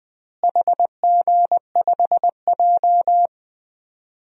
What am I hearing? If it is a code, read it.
Morse code, heard as a keyed tone, about 20 wpm, 700 Hz. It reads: HG5J
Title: DXCC-100 Call Signs - 4 Characters - Repeated 3x - 20wpm